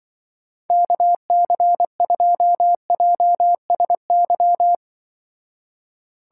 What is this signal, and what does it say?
Morse code, heard as a keyed tone, about 24 wpm, 700 Hz. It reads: KC2JSY
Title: Speed Racing - USA Call Signs 20wpm